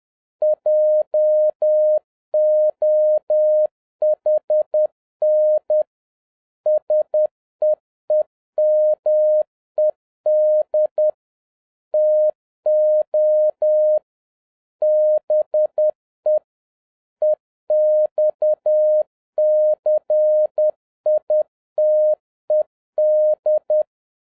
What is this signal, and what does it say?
Morse code, heard as a keyed tone, about 10 wpm, 610 Hz. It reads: JOHN SEEMED TO BE EXCITED
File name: 2011-07-mp3 / n46 JOHN K3TN 10wpm